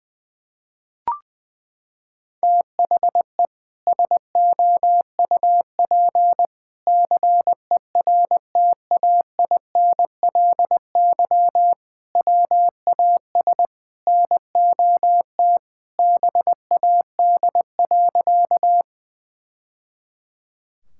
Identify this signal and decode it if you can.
Morse code, heard as a keyed tone, about 20 wpm, 700 Hz. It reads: THE SOUP CERTAINLY WAS NOT BAD.
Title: Sentences from Top 1000 Words - Review Part 6 - 20wpm